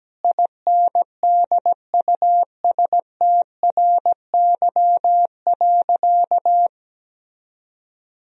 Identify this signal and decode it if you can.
Morse code, heard as a keyed tone, about 17 wpm, 700 Hz. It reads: INDUSTRY.
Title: Sentences from Top 1000 Words - Review Part 4 - 17wpm